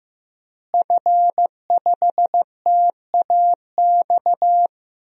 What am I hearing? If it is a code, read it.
Morse code, heard as a keyed tone, about 15 wpm, 700 Hz. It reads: F5TAX
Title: DXCC-100 Call Signs - 5 Characters - Repeated 2x - 15wpm